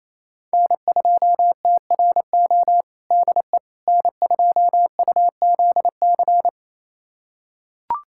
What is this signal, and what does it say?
Morse code, heard as a keyed tone, about 28 wpm, 700 Hz. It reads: N2TRO DE N2UZC
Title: USA Callsign DE Callsign - 5 Characters - 28wpm